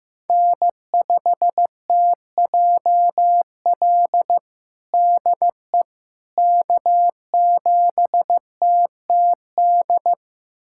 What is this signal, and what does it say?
Morse code, heard as a keyed tone, about 15 wpm, 700 Hz. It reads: N5TJL DE K7TTD
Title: USA Callsign DE Callsign - 5 Characters - 15wpm